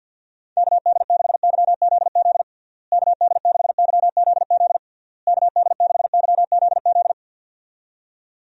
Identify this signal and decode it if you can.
Morse code, heard as a keyed tone, about 51 wpm, 700 Hz. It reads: KD6YZZ KD6YZZ KD6YZZ
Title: USA Call Signs - 6 Characters - Repeated 3x - 50wpm